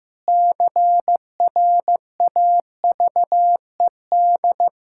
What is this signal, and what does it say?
Morse code, heard as a keyed tone, about 15 wpm, 700 Hz. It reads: CRAVED
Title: Pangram Challenge v2 - 15wpm